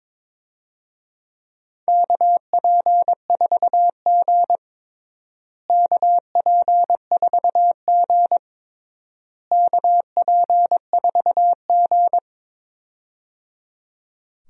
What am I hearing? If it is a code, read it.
Morse code, heard as a keyed tone, about 21 wpm, 700 Hz. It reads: KP4G KP4G KP4G
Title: DXCC-100 Call Signs - 4 Characters - Repeated 3x - 22wpm